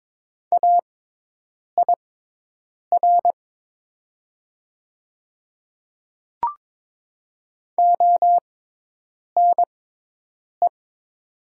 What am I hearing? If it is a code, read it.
Morse code, heard as a keyed tone, about 23 wpm, spaced as for 8 wpm, 700 Hz. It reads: AIR ONE
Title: 6x Character Spacing - Top 200 Words - 22wpm